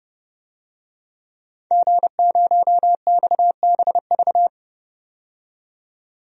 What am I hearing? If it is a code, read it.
Morse code, heard as a keyed tone, about 30 wpm, 700 Hz. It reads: G0XBV
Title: DXCC-100 Call Signs - 5 Characters - 30wpm